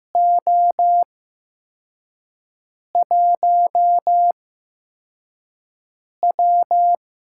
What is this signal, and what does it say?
Morse code, heard as a keyed tone, about 15 wpm, spaced as for 4 wpm, 700 Hz. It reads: O1W
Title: USA Call Signs - 4 Characters - 8x Character Spacing - 15wpm